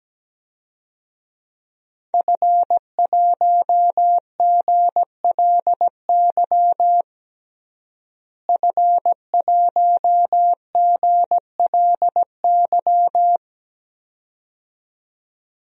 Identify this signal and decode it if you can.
Morse code, heard as a keyed tone, about 17 wpm, 700 Hz. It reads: F1GLY F1GLY
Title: DXCC-100 Call Signs - 5 Characters - Repeated 2x - 17wpm